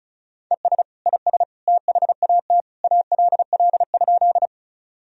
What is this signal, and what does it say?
Morse code, heard as a keyed tone, about 36 wpm, 700 Hz. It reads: ES IS THAT ALL?
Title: Sentences from Top 100 Words and Abbreviations 35wpm